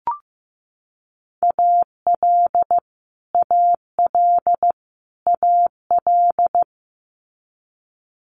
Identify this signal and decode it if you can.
Morse code, heard as a keyed tone, about 15 wpm, 700 Hz. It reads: AL AL AL
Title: Multi-letter Phonemes Repeated 3x 15wpm